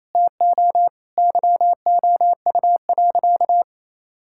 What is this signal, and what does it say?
Morse code, heard as a keyed tone, about 28 wpm, 700 Hz. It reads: TO YOU.
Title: Sentences from Top 1600 Words 28wpm